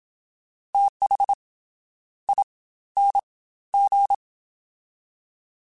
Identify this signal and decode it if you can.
Morse code, heard as a keyed tone, about 27 wpm, 800 Hz. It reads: TH I N G
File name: TH / TH mp3s / Fast TH @25